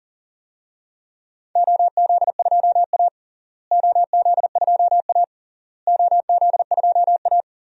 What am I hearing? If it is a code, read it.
Morse code, heard as a keyed tone, about 37 wpm, 700 Hz. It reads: OZ2A OZ2A OZ2A
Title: DXCC-100 Call Signs - 4 Characters - Repeated 3x - 40wpm